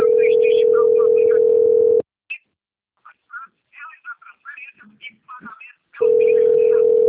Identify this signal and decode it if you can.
Ringback tone (North America), 440 + 480 Hz, with speech.
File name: out- - - - - .33388